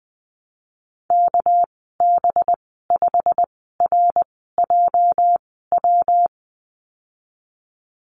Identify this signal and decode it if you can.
Morse code, heard as a keyed tone, about 20 wpm, spaced as for 15 wpm, 700 Hz. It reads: KB5RJW
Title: 213 Lesson 34g - Number 8 - Call Signs 20wpm